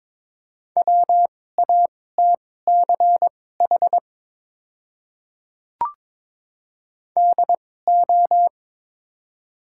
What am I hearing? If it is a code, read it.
Morse code, heard as a keyed tone, about 22 wpm, spaced as for 16 wpm, 700 Hz. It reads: WATCH DO
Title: 2x Character Spacing - Top 500 Words - 22wpm